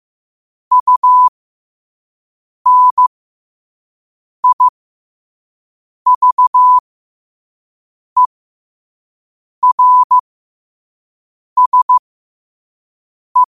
Morse code, heard as a keyed tone, about 15 wpm, spaced as for 5 wpm, 1000 Hz. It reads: UNIVERSE
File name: qotdfiles / 5wpm